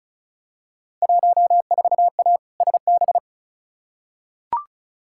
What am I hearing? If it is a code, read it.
Morse code, heard as a keyed tone, about 36 wpm, 690 Hz. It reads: 14A SB
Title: Field Day Exchanges Encore - 35wpm